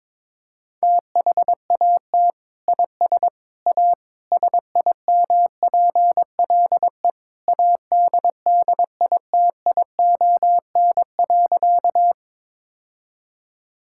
Morse code, heard as a keyed tone, about 21 wpm, 700 Hz. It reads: THAT IS A SIMPLE ADDITION.
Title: Sentences from Top 1000 Words - Review Part 6 - 22wpm